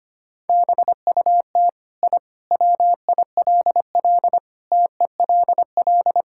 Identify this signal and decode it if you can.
Morse code, heard as a keyed tone, about 25 wpm, 700 Hz. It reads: BUT I WILL TELL
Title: Sentences from Top 1000 Words - Review Part 5 - 25wpm